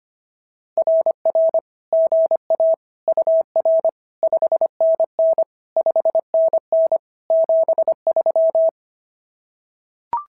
Morse code, heard as a keyed tone, about 25 wpm, 650 Hz. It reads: RR GA UR 5NN 5NN 73
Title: SOTA QSO Practice 25wpm